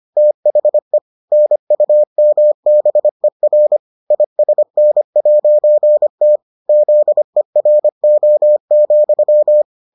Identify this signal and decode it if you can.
Morse code, heard as a keyed tone, about 25 wpm, 600 Hz. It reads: THE NUMBER ISN'T ZERO,